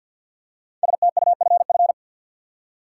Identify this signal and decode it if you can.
Morse code, heard as a keyed tone, about 53 wpm, 700 Hz. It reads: STUFF
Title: Five Letter Words - 50wpm